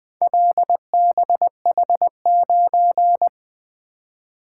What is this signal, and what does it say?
Morse code, heard as a keyed tone, about 24 wpm, 700 Hz. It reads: LBH9
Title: Speed-Racing - Four Letter-Number Combinations - 20wpm